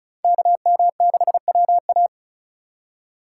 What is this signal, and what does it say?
Morse code, heard as a keyed tone, about 46 wpm, 700 Hz. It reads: KM6WA